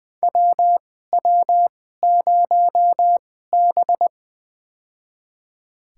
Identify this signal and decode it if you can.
Morse code, heard as a keyed tone, about 20 wpm, spaced as for 14 wpm, 700 Hz. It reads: WW0B
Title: USA Call Signs - 4 Characters - 2x Character Spacing - 20wpm